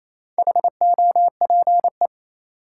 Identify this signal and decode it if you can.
Morse code, heard as a keyed tone, about 28 wpm, 700 Hz. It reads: HOPE
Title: Sentences from Top 1000 Words - New Word Drill - 28wpm